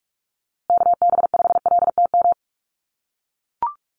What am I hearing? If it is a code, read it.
Morse code, heard as a keyed tone, about 44 wpm, 700 Hz. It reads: KB5LTM
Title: USA Call Signs Repeated 2x 45wpm